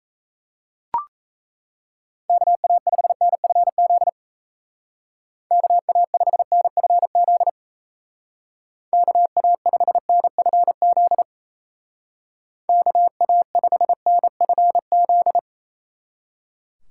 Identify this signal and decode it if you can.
Morse code, heard as a keyed tone, about 34 wpm, 700 Hz. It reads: KA5NFZ KA5NFZ KA5NFZ KA5NFZ
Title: Speed Racing - USA 2x3 Call Signs 28wpm